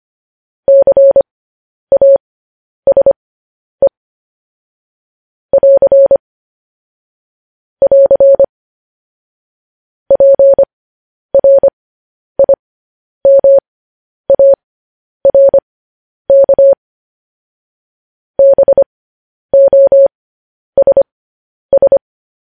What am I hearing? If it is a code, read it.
Morse code, heard as a keyed tone, about 25 wpm, spaced as for 10 wpm, 550 Hz. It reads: CASE <AR> <AR> PRIMARK BOSS
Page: news-headlines-in-morse-code-at-10-wpm